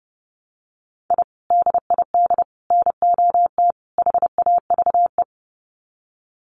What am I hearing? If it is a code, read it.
Morse code, heard as a keyed tone, about 30 wpm, 700 Hz. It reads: I DID NOT HAVE
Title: Sets of 4 Words 30wpm